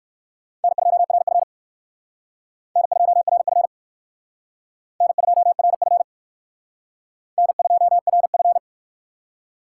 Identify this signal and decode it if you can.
Morse code, heard as a keyed tone, about 58 wpm, 700 Hz. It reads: N2RF N2RF N2RF N2RF